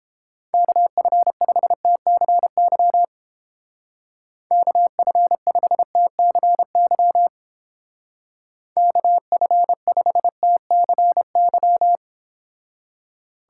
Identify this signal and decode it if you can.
Morse code, heard as a keyed tone, about 30 wpm, 700 Hz. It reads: KF5TCY KF5TCY KF5TCY